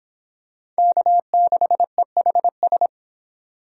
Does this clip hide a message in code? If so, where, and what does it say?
Yes: Morse code from 0.8 to 2.9 s, heard as a keyed tone, about 24 wpm, 700 Hz. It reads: K6EHS